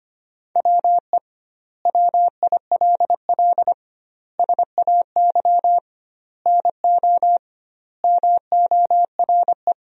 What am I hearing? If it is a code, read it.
Morse code, heard as a keyed tone, about 25 wpm, 700 Hz. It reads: WE WILL SAY NO MORE
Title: 2x Word-Spacing - Sentences from Top 200 Words - 25wpm